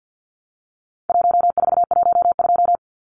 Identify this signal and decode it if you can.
Morse code, heard as a keyed tone, about 50 wpm, 700 Hz. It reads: 1412
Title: Numbers - 4-Digit 50wpm